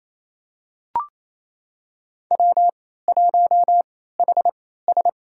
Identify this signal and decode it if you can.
Morse code, heard as a keyed tone, about 28 wpm, spaced as for 16 wpm, 700 Hz. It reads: W1HS
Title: USA Call Signs - 3x Character Spacing - 28wpm